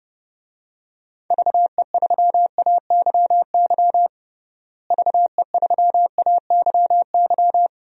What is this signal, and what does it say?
Morse code, heard as a keyed tone, about 30 wpm, 700 Hz. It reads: VE3AYY VE3AYY
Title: DXCC-100 Call Signs - 6 Characters - Repeated 3x - 30wpm